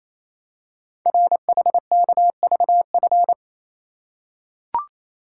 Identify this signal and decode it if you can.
Morse code, heard as a keyed tone, about 27 wpm, 700 Hz. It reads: RHKVF
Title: Code Groups - 5 Letters - 28wpm